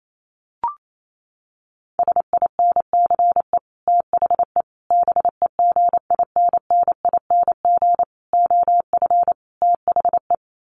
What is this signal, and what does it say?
Morse code, heard as a keyed tone, about 28 wpm, 700 Hz. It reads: SINCE THE BEGINNING OF THE
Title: Sets of 5 Words Encore2 28wpm